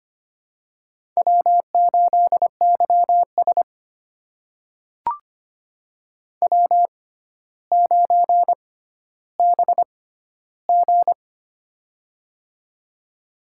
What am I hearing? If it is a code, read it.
Morse code, heard as a keyed tone, about 25 wpm, spaced as for 13 wpm, 700 Hz. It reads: W8YS W9BG
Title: USA Call Signs - 4 Characters - 6x Character Spacing - 25wpm